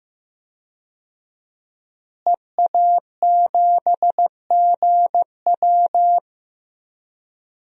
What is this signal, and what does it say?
Morse code, heard as a keyed tone, about 15 wpm, 700 Hz. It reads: EA7GW
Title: DXCC-100 Call Signs - 5 Characters - Repeated 2x - 15wpm